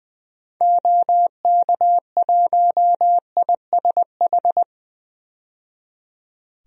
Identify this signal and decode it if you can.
Morse code, heard as a keyed tone, about 21 wpm, 700 Hz. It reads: OK1ISH
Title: DXCC-100 Call Signs - 6 Characters - 20wpm